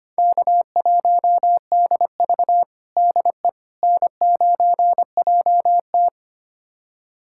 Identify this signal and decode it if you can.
Morse code, heard as a keyed tone, about 25 wpm, 700 Hz. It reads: K1DV DE N9JT
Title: USA Callsign DE Callsign - 4 Characters - 25wpm